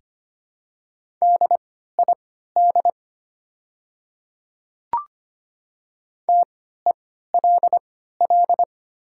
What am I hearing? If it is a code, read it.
Morse code, heard as a keyed tone, about 25 wpm, spaced as for 14 wpm, 700 Hz. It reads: DID TELL
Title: 3x Character Spacing - Top 200 Words - 25wpm